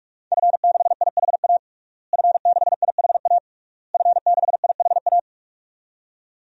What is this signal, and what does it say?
Morse code, heard as a keyed tone, about 46 wpm, 700 Hz. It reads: F6IHA F6IHA F6IHA
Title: DXCC-100 Call Signs - 5 Characters - Repeated 3x - 45wpm